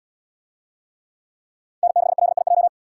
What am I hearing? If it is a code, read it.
Morse code, heard as a keyed tone, about 77 wpm, 700 Hz. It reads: N6LIJ